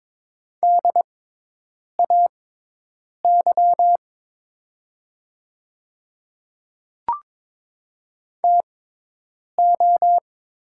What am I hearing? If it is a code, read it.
Morse code, heard as a keyed tone, about 22 wpm, spaced as for 8 wpm, 700 Hz. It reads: DAY TO
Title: 6x Character Spacing - Top 100 Words - 22wpm